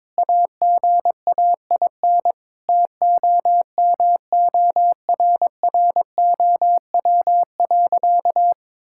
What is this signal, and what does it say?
Morse code, heard as a keyed tone, about 22 wpm, 700 Hz. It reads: AGAIN TOMORROW.
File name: practice / 22wpm / Sentences with 4-Word Phrases - 22wpm